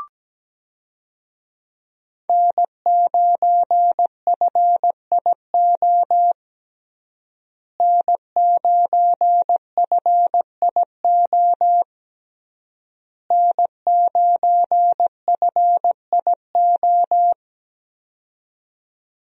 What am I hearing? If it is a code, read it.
Morse code, heard as a keyed tone, about 17 wpm, 700 Hz. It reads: N9FIO N9FIO N9FIO
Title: USA Call Signs - 5 Characters - Repeated 3x - 17wpm